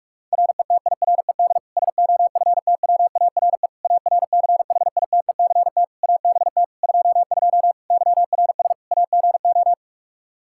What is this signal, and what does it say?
Morse code, heard as a keyed tone, about 46 wpm, 700 Hz. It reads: RETIRED SOFTWARE ARCHITECT ABT 22 YRS AGO
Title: Practice QSO - Exchange 3 - More Info - 45wpm